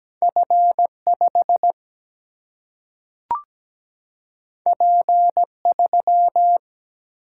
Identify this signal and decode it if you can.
Morse code, heard as a keyed tone, about 17 wpm, 700 Hz. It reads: F5 P3
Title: Single-Letter Single-Number 17wpm